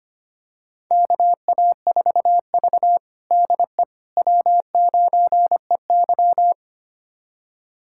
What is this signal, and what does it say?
Morse code, heard as a keyed tone, about 25 wpm, 700 Hz. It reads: KA4V DE W9EY
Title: USA Callsign DE Callsign - 4 Characters - 25wpm